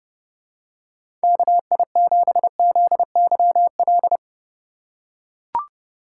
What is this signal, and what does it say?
Morse code, heard as a keyed tone, about 31 wpm, 700 Hz. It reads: KI7ZYL
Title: USA Call Signs - 6 Characters - 4x Character Spacing - 30wpm